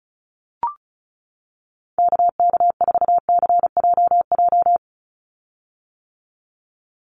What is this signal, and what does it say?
Morse code, heard as a keyed tone, about 34 wpm, 700 Hz. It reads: KK4CJJ
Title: USA Call Signs 35wpm